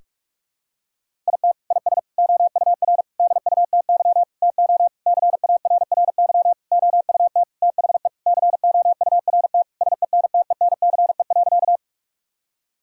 Morse code, heard as a keyed tone, about 41 wpm, 700 Hz. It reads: IT IS OUR DUTY TO CARRY OUT THE COURT SENTENCE.
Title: Sentences from Top 1600 Words 45wpm